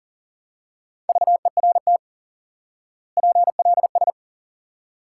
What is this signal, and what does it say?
Morse code, heard as a keyed tone, about 37 wpm, 700 Hz. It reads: VERT PLS
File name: IQE Course 3 - Lesson 89 - YAGI - Two Elements - 40wpm